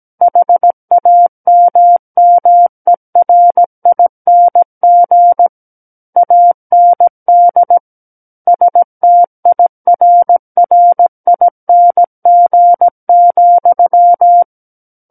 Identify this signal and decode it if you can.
Morse code, heard as a keyed tone, about 17 wpm, 700 Hz. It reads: HAMMERING AND STIRRING,
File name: Book One / 08-Chapter 8 - 17 WPM